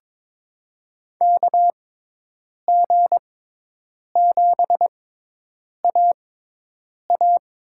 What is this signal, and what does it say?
Morse code, heard as a keyed tone, about 22 wpm, spaced as for 8 wpm, 700 Hz. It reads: KG7AA
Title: USA Call Signs - 6 Characters - 6x Character Spacing - 22wpm